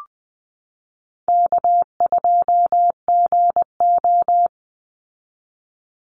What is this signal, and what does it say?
Morse code, heard as a keyed tone, about 20 wpm, 700 Hz. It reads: K2GO
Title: Call Signs Repeated 3x - 20wpm